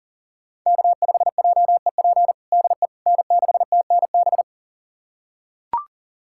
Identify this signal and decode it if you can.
Morse code, heard as a keyed tone, about 40 wpm, 700 Hz. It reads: K5JEP DE N6TNB
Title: USA Callsign DE Callsign - 5 Characters - 40wpm